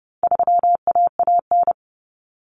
Morse code, heard as a keyed tone, about 30 wpm, 700 Hz. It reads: 3AAN
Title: Common Cut Numbers - 4-Digit 30wpm